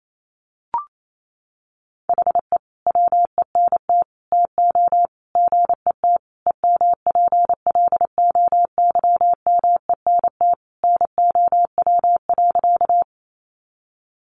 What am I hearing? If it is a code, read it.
Morse code, heard as a keyed tone, about 28 wpm, 700 Hz. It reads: HE WENT TO GET EMPLOYMENT NOW.
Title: Sentences from Top 1500 Words 28wpm